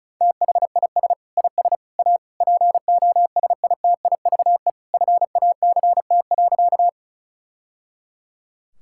Morse code, heard as a keyed tone, about 36 wpm, 700 Hz. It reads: THIS IS A POSITIVE FACT.
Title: Sentences from Top 1000 Words - Review Part 5 - 35wpm